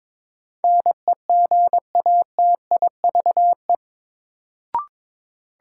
Morse code, heard as a keyed tone, about 21 wpm, 700 Hz. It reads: NEGATIVE
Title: Sentences from Top 1000 Words - New Word Drill - 22wpm